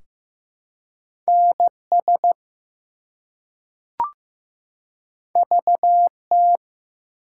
Morse code, heard as a keyed tone, about 15 wpm, 700 Hz. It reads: NS VT
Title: US States and Canadian Provinces 15wpm